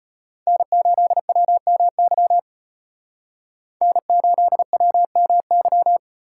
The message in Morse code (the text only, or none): N8WMY N8WMY